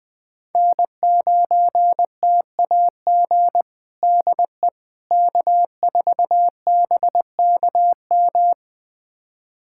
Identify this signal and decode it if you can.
Morse code, heard as a keyed tone, about 21 wpm, 700 Hz. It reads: N9TAG DE K4BKM
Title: USA Callsign DE Callsign - 5 Characters - 20wpm